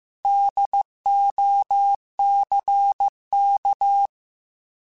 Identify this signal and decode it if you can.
Morse code, heard as a keyed tone, about 15 wpm, 800 Hz. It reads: DOCK
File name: Alphabetical / Alphabetical mp3s / Ds @13